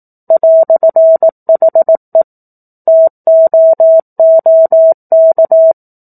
Morse code, heard as a keyed tone, about 18 wpm, 650 Hz. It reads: "HE TOOK
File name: 04-Chapter-04 18 WPM